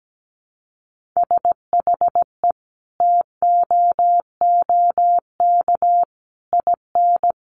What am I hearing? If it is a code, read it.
Morse code, heard as a keyed tone, about 17 wpm, 700 Hz. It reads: SHE TOOK IN
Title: Sentences from Top 1500 Words 17wpm